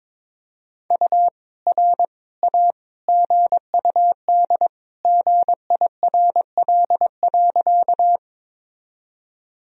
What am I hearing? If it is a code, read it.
Morse code, heard as a keyed tone, about 22 wpm, 700 Hz. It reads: U R A GUD GIRL.
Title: Sentences from Top 400 Words and Abbreviations 22wpm